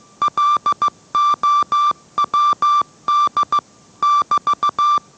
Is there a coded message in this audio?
Yes: Morse code from 0.2 to 5.0 s, heard as a keyed tone, about 15 wpm, 1170 Hz. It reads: LOWD <BT>